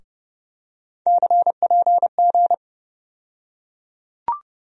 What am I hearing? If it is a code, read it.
Morse code, heard as a keyed tone, about 30 wpm, 700 Hz. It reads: CPG
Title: USA Call Sign Postfixes - 3 Letters 30wpm